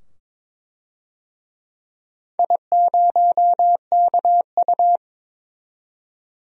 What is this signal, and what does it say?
Morse code, heard as a keyed tone, about 22 wpm, 700 Hz. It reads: I0KU
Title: DXCC-100 Call Signs - 4 Characters - 22wpm